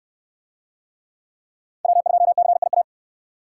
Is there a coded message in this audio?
Yes: Morse code from 1.8 to 2.8 s, heard as a keyed tone, about 67 wpm, 700 Hz. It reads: W2CIA